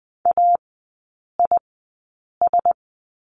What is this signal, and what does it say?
Morse code, heard as a keyed tone, about 20 wpm, spaced as for 8 wpm, 700 Hz. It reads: AIS
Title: 107 Lesson 18e - Letter m - Three Characters 20wpm